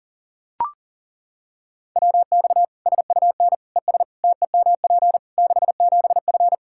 Morse code, heard as a keyed tone, about 37 wpm, 700 Hz. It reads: WX SUN ES TEMP 67F
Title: Practice QSO - Exchange 2 - Info - 40wpm